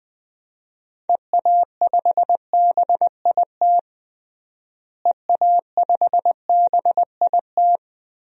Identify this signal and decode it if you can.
Morse code, heard as a keyed tone, about 21 wpm, 700 Hz. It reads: EA5BIT EA5BIT
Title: DXCC-100 Call Signs - 6 Characters - Repeated 2x - 20wpm